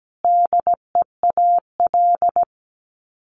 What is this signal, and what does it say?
Morse code, heard as a keyed tone, about 17 wpm, 700 Hz. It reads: DEAL
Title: Sets of 3 Words 17wpm